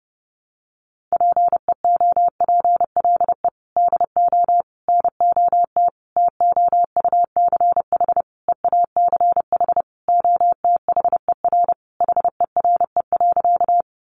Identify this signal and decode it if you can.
Morse code, heard as a keyed tone, about 28 wpm, 700 Hz. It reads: PEOPLE DO NOT TOUCH EACH OTHER HERE.
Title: Sentences from Top 1200 Words 30wpm